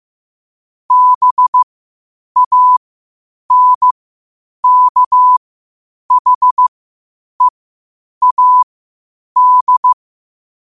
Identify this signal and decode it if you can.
Morse code, heard as a keyed tone, about 15 wpm, spaced as for 8 wpm, 1000 Hz. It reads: BANKHEAD
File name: qotdfiles / 8wpm